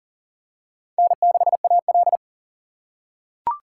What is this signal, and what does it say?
Morse code, heard as a keyed tone, about 40 wpm, 700 Hz. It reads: N6AL
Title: USA Call Signs - 4 Characters - 2x Character Spacing - 40wpm